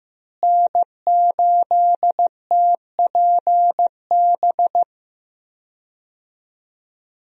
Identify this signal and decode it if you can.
Morse code, heard as a keyed tone, about 15 wpm, 700 Hz. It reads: N8TPB
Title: USA Call Signs - 5 Characters - Repeated 3x - 15wpm